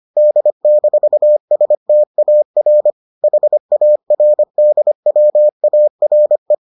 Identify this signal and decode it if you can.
Morse code, heard as a keyed tone, about 25 wpm, 600 Hz. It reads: D-STAR HARDWARE